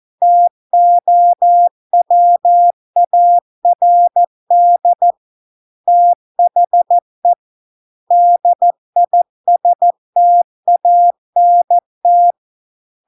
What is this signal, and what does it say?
Morse code, heard as a keyed tone, about 14 wpm, 700 Hz. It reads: TOWARD THE DISTANT